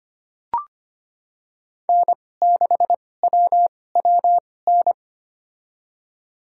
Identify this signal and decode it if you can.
Morse code, heard as a keyed tone, about 25 wpm, spaced as for 18 wpm, 700 Hz. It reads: N6WWN
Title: USA Call Signs - 5 Characters - 2x Character Spacing - 25wpm